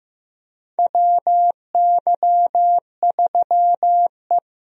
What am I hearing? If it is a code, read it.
Morse code, heard as a keyed tone, about 15 wpm, 700 Hz. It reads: WY3E
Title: USA Call Signs - 4 Characters - Repeated 3x - 15wpm